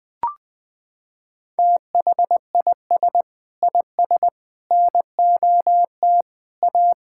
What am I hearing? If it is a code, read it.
Morse code, heard as a keyed tone, about 21 wpm, 700 Hz. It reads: THIS IS NOT A
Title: Sentences from Top 1000 Words - Review Part 6 - 20wpm